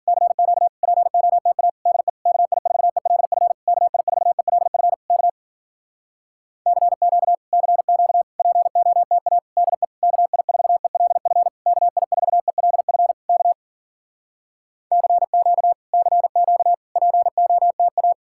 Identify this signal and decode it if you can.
Morse code, heard as a keyed tone, about 52 wpm, 700 Hz. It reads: CQ POTA DE KI4ELF KI4ELF K CQ CQ POTA DE KI4ELF KI4ELF K CQ CQ POTA